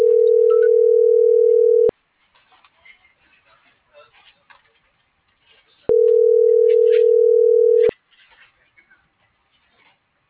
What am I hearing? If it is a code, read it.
Ringback tone (North America), 440 + 480 Hz.